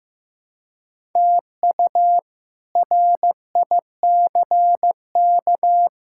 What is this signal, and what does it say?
Morse code, heard as a keyed tone, about 15 wpm, 700 Hz. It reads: TU RICK
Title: K1USN SST QSO Practice - 15wpm